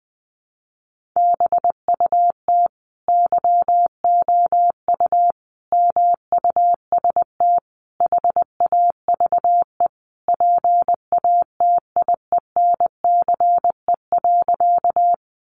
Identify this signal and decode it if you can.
Morse code, heard as a keyed tone, about 19 wpm, 700 Hz. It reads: BUT YOU MUST HAVE PATIENCE.
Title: Sentences from Top 1300 Words 20wpm